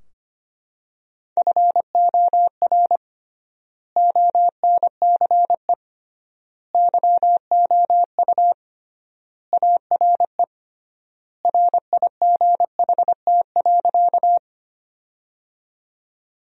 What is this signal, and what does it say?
Morse code, heard as a keyed tone, about 25 wpm, 700 Hz. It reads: FOR ONCE YOU ARE RIGHT.
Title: 3x Word-Spacing - Sentences from Top 300 Words - 25wpm